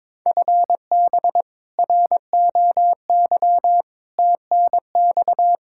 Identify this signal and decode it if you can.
Morse code, heard as a keyed tone, about 22 wpm, 700 Hz. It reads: FB ROY TNX
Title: K1USN SST QSO Practice 22wpm